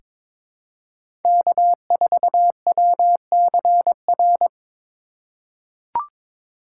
Morse code, heard as a keyed tone, about 21 wpm, 700 Hz. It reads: K4WCR
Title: USA Call Signs - 5 Characters - 3x Character Spacing - 22wpm